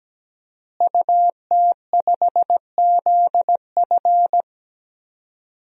Morse code, heard as a keyed tone, about 17 wpm, 700 Hz. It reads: UT5ZF
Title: DXCC-100 Call Signs - 5 Characters - Repeated 3x - 17wpm